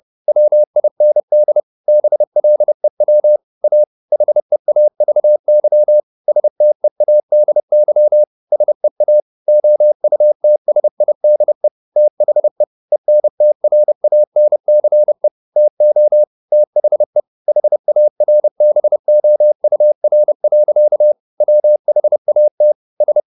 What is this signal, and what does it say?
Morse code, heard as a keyed tone, about 30 wpm, 600 Hz. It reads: WIND BLEW A HEAVY STEADY SEA OUTSIDE THE ENTRANCE TO THE HARBOUR. WHAT S